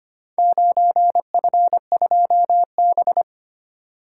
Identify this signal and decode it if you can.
Morse code, heard as a keyed tone, about 25 wpm, 700 Hz. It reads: 9F2B
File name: practice / 25wpm / Four Letter-Number Combinations - 25wpm